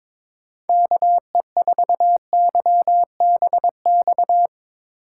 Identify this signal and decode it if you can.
Morse code, heard as a keyed tone, about 22 wpm, 700 Hz. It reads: KE4YBX
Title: USA Call Signs - 6 Characters - 4x Character Spacing - 22wpm